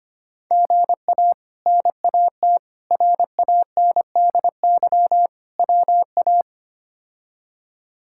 Morse code, heard as a keyed tone, about 25 wpm, 700 Hz. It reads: GA NAT RANDY WA
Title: K1USN SST QSO Practice - 25wpm